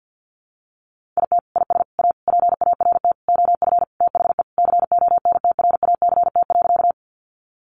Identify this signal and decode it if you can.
Morse code, heard as a keyed tone, about 48 wpm, 700 Hz. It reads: IT IS A PART OF THE CONTRACT.
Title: Sentences from Top 1400 Words 50wpm